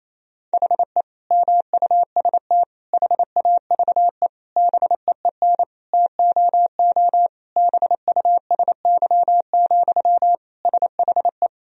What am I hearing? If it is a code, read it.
Morse code, heard as a keyed tone, about 28 wpm, 700 Hz. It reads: HE MUST HAVE BEEN TOO BUSY, SHE
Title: Sentences from Top 1000 Words - Review Part 3 - 28wpm